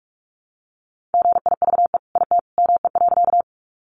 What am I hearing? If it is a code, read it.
Morse code, heard as a keyed tone, about 44 wpm, 700 Hz. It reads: GIVE IT ME.
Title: Sentences from Top 200 Words 45wpm